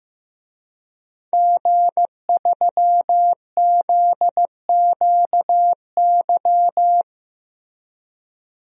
Morse code, heard as a keyed tone, about 15 wpm, 700 Hz. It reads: G3ZQY
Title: DXCC-100 Call Signs - 5 Characters - Repeated 3x - 15wpm